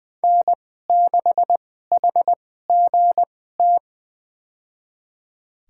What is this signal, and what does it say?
Morse code, heard as a keyed tone, about 21 wpm, spaced as for 15 wpm, 700 Hz. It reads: N6HGT
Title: USA Call Signs - 5 Characters - 2x Character Spacing - 20wpm